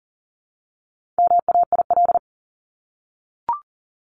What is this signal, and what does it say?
Morse code, heard as a keyed tone, about 40 wpm, 700 Hz. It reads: MAIL
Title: Top 1751-2000 Words 40wpm